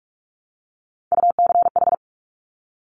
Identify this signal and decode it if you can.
Morse code, heard as a keyed tone, about 46 wpm, 700 Hz. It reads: UCH
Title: Top Three Letter Combinations in Words 45wpm